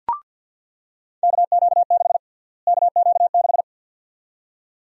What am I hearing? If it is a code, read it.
Morse code, heard as a keyed tone, about 46 wpm, 700 Hz. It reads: KQ6 KQ6
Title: USA Call Sign Prefixes - Repeated 2x - 50wpm